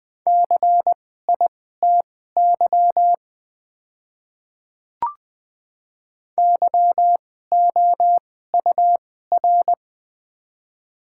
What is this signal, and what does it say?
Morse code, heard as a keyed tone, about 21 wpm, spaced as for 15 wpm, 700 Hz. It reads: CITY YOUR
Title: 2x Character Spacing - Top 300 Words - 20wpm